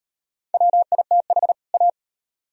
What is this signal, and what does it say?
Morse code, heard as a keyed tone, about 34 wpm, 700 Hz. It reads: WITH A